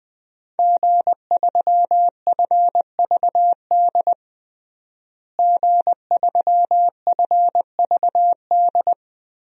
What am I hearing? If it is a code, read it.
Morse code, heard as a keyed tone, about 20 wpm, 700 Hz. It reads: G3FVD G3FVD
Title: DXCC-100 Call Signs - 5 Characters - Repeated 3x - 20wpm